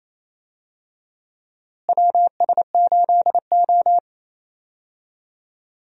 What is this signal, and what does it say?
Morse code, heard as a keyed tone, about 28 wpm, 700 Hz. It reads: WS8O
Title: USA Call Signs - 4 Characters - Repeated 2x - 28wpm